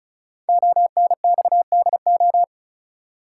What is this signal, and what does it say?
Morse code, heard as a keyed tone, about 33 wpm, 700 Hz. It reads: ONXDO
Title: Code Groups - 5 Letters - 35wpm